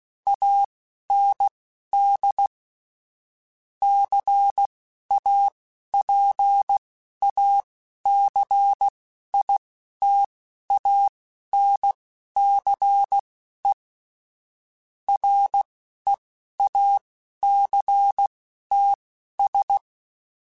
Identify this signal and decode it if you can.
Morse code, heard as a keyed tone, about 16 wpm, spaced as for 11 wpm, 800 Hz. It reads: AND CAPACITANCE REACTS
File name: Technical Subjects / Technical Subjects-mp3s / Reactance @10